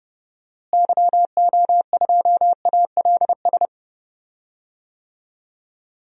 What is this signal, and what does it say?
Morse code, heard as a keyed tone, about 30 wpm, 700 Hz. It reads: YO2ALS
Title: DXCC-100 Call Signs - 6 Characters - Repeated 3x - 30wpm